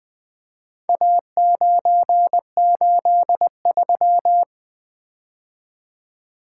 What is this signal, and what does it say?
Morse code, heard as a keyed tone, about 20 wpm, 700 Hz. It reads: A983
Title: Four Letter-Number Combinations - 20wpm